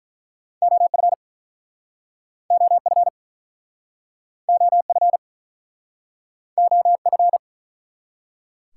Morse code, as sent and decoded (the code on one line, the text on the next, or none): --- ..-. / --- ..-. / --- ..-. / --- ..-.
OF OF OF OF